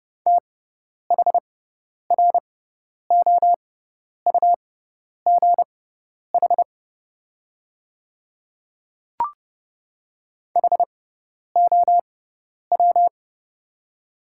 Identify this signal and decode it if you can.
Morse code, heard as a keyed tone, about 30 wpm, spaced as for 10 wpm, 690 Hz. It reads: THROUGH HOW
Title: 6x Character Spacing - Top 500 Words - 30wpm